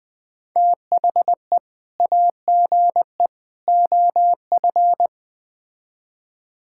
Morse code, heard as a keyed tone, about 20 wpm, 700 Hz. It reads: THE AGE OF
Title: Build A Phrase - 3-Word Phrases - 20wpm